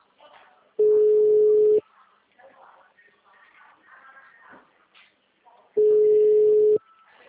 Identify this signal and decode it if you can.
Ringback tone (Europe, 425 Hz), with speech.